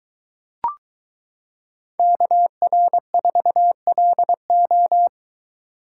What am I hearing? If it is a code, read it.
Morse code, heard as a keyed tone, about 22 wpm, 700 Hz. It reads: KR4LO